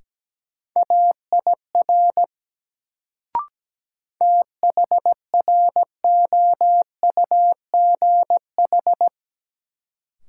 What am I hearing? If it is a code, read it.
Morse code, heard as a keyed tone, about 17 wpm, 700 Hz. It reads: AIR THROUGH
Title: Sets of 2 Words - New Word Drill - 17wpm